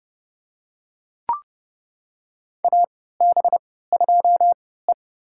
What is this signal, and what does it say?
Morse code, heard as a keyed tone, about 30 wpm, spaced as for 17 wpm, 700 Hz. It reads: AB2E
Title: USA Call Signs - 4 Characters - 3x Character Spacing - 30wpm